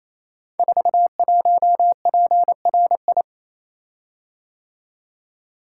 Morse code, heard as a keyed tone, about 27 wpm, 700 Hz. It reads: 41PRI
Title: Code Groups - 5 Letter:Numbers - 28wpm